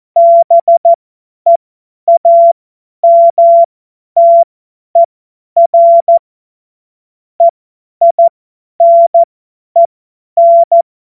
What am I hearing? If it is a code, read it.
Morse code, heard as a keyed tone, about 14 wpm, spaced as for 10 wpm, 680 Hz. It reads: BEAMTER EINEN
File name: Witz-23-September-2025-50-F70